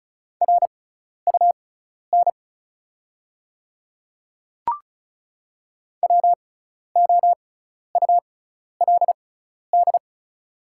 Morse code, heard as a keyed tone, about 36 wpm, spaced as for 12 wpm, 690 Hz. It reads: RUN WOULD
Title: 6x Character Spacing - Top 500 Words - 35wpm